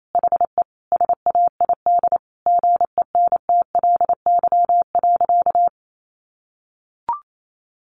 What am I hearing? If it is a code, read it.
Morse code, heard as a keyed tone, about 28 wpm, 700 Hz. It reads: HE SAID GENTLY.
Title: Sentences from Top 1500 Words 28wpm